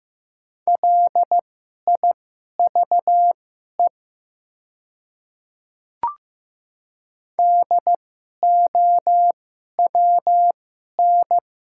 Morse code, heard as a keyed tone, about 15 wpm, spaced as for 11 wpm, 700 Hz. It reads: LIVE DOWN
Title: 2x Character Spacing - Top 200 Words - 15wpm